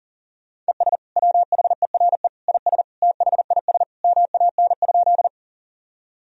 Morse code, heard as a keyed tone, about 40 wpm, 700 Hz. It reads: ES WHERE IS THIS MAN?
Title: Sentences from Top 200 Words and Abbreviations 40wpm